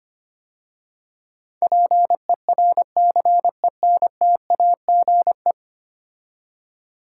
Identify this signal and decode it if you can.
Morse code, heard as a keyed tone, about 25 wpm, 690 Hz. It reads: PERCENTAGE
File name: 10 Letter Words - First 3 Letters Given - 25wpm